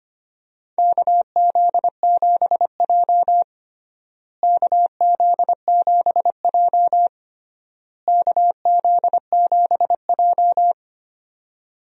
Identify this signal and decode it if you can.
Morse code, heard as a keyed tone, about 25 wpm, 700 Hz. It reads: KZ7J KZ7J KZ7J
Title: USA Call Signs - 4 Characters - Repeated 3x - 25wpm